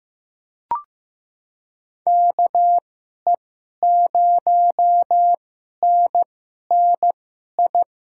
Morse code, heard as a keyed tone, about 15 wpm, spaced as for 11 wpm, 700 Hz. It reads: KE0NNI
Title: USA Call Signs - 2x Character Spacing - 15wpm